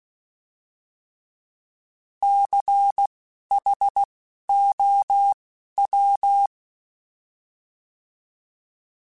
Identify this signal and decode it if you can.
Morse code, heard as a keyed tone, about 16 wpm, spaced as for 11 wpm, 800 Hz. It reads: CHOW
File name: Alphabetical / Alphabetical mp3s / Cs @10